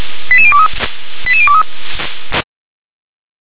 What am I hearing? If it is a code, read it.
Five-tone selective call (ZVEI), heard as quick runs of five tones, tones of about 70 ms. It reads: 88023 88023
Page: SDR/ZVEI